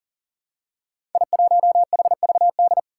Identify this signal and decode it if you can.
Morse code, heard as a keyed tone, about 40 wpm, 700 Hz. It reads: I1HVD
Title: Five Challenging Letter-Number Combinations 40wpm